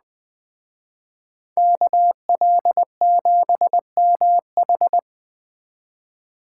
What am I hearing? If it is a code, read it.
Morse code, heard as a keyed tone, about 20 wpm, 700 Hz. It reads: KL7MH